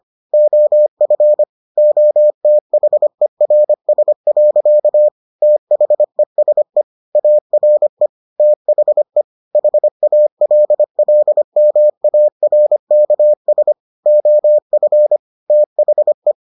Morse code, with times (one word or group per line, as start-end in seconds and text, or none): 0.3-1.4 OF
1.8-5.1 OTHERS.
5.4-6.8 THESE
7.2-8.1 ARE
8.4-9.2 THE
9.6-13.7 HALLMARKS
14.1-15.2 OF
15.5-16.3 THE